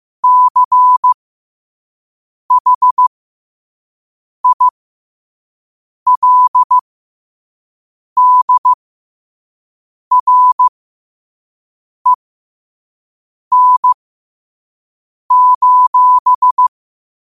Morse code, heard as a keyed tone, about 15 wpm, spaced as for 5 wpm, 1000 Hz. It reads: CHILDREN:
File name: qotdfiles / 5wpm